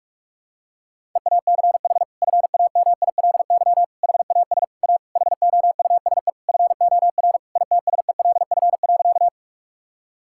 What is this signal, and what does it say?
Morse code, heard as a keyed tone, about 46 wpm, 700 Hz. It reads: EACH FAMILY HAS A HOUSE FOR ITSELF.
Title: Sentences from Top 1000 Words - Review Part 5 - 45wpm